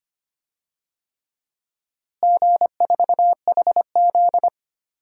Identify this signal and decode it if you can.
Morse code, heard as a keyed tone, about 25 wpm, 700 Hz. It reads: G4HZ
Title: DXCC-100 Call Signs - 4 Characters - Repeated 3x - 25wpm